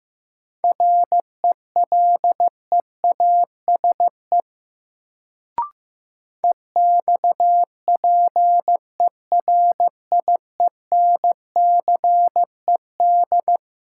Morse code, heard as a keyed tone, about 15 wpm, 700 Hz. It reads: RELEASE EXPERIENCED
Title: Sentences from Top 1100 Words - New Word Drill Alt - 15wpm